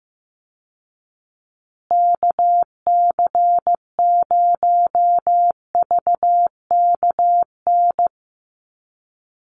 Morse code, heard as a keyed tone, about 15 wpm, 700 Hz. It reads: KC0VKN
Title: Call Signs Repeated 2x - 15wpm